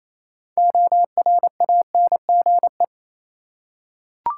Morse code, heard as a keyed tone, about 28 wpm, 700 Hz. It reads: ORANGE
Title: Six Letter Words - 28wpm